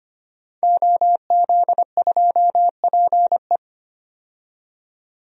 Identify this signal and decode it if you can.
Morse code, heard as a keyed tone, about 25 wpm, 700 Hz. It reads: OZ2PE
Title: DXCC-100 Call Signs - 5 Characters - 25wpm